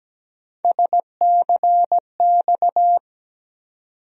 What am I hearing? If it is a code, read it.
Morse code, heard as a keyed tone, about 17 wpm, 700 Hz. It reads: SCX
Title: USA Call Sign Postfixes - 3 Letters 17wpm